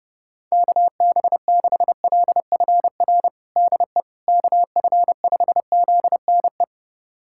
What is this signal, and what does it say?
Morse code, heard as a keyed tone, about 30 wpm, 700 Hz. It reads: KB6LFR DE KF5ZNE
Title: USA Callsign DE Callsign 30wpm